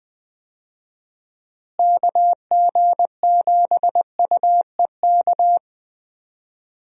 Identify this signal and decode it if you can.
Morse code, heard as a keyed tone, about 20 wpm, 700 Hz. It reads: KG7UEK
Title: USA Call Signs - 6 Characters - Repeated 3x - 20wpm